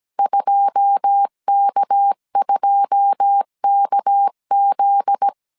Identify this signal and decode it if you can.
Morse code, heard as a keyed tone, about 17 wpm, 800 Hz. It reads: 2K2KZ
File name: CW / DET / 1 9 1500C S C 16wpm 2909